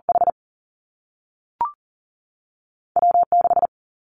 Morse code, heard as a keyed tone, about 38 wpm, 700 Hz. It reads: H W 6
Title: Two Challenging Letter-Number Combinations 40wpm